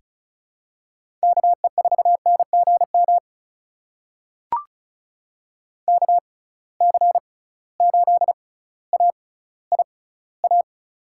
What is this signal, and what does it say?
Morse code, heard as a keyed tone, about 36 wpm, spaced as for 18 wpm, 700 Hz. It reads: KE4NGM KC8AIA
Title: USA Call Signs - 6 Characters - 6x Character Spacing - 35wpm